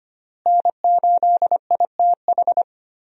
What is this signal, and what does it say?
Morse code, heard as a keyed tone, about 25 wpm, 700 Hz. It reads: N8ITH
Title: USA Call Signs - 5 Characters - Repeated 3x - 25wpm